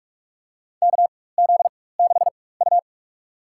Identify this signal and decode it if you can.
Morse code, heard as a keyed tone, about 47 wpm, spaced as for 21 wpm, 700 Hz. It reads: KZ6U
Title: USA Call Signs - 4 Characters - 4x Character Spacing - 45wpm